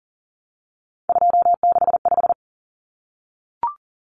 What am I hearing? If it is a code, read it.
Morse code, heard as a keyed tone, about 40 wpm, 700 Hz. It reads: 265
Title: Numbers - 3-Digit 40wpm